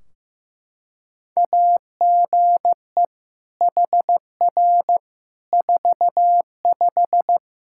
Morse code, heard as a keyed tone, about 15 wpm, 700 Hz. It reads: AGE HR 45
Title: K1USN SST QSO Practice 15wpm